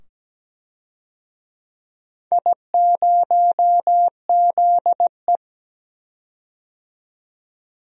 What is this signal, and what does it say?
Morse code, heard as a keyed tone, about 17 wpm, 700 Hz. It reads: I0ZE
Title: DXCC-100 Call Signs - 4 Characters - 17wpm